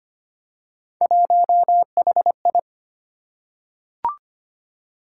Morse code, heard as a keyed tone, about 25 wpm, 700 Hz. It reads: 1HI
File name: Three Letter-Number Combinations - 25wpm